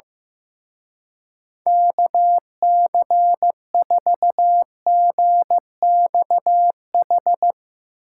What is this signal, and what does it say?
Morse code, heard as a keyed tone, about 15 wpm, 700 Hz. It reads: KC4GXH
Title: Speed Racing - USA 2x3 Call Signs 15wpm